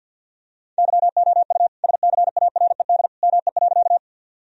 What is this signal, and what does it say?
Morse code, heard as a keyed tone, about 46 wpm, 700 Hz. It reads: YOU SCARED ME.
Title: Sentences from Top 1000 Words - Review Part 4 - 50wpm